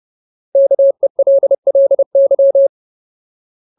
Morse code, heard as a keyed tone, about 30 wpm, 550 Hz. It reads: KELLY
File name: Ultimate Word List / 30WPM Ultimate